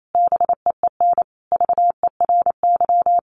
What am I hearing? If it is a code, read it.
Morse code, heard as a keyed tone, about 28 wpm, 700 Hz. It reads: BEEN VERY
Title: Sentences from Top 1500 Words 28wpm